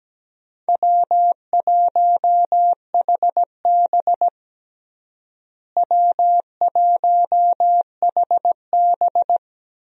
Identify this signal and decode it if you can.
Morse code, heard as a keyed tone, about 17 wpm, 700 Hz. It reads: W1HB W1HB
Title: USA Call Signs - 4 Characters - Repeated 3x - 17wpm